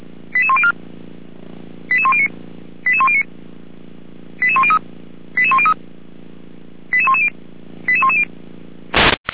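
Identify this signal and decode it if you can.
Five-tone selective call (ZVEI), heard as quick runs of five tones, tones of about 70 ms. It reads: 88104 88109 88109 88103 88103 88110 88110
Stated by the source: ZVEI five-tone call received by SDR, from server of Feuerwehr Orlamünde (fire brigade).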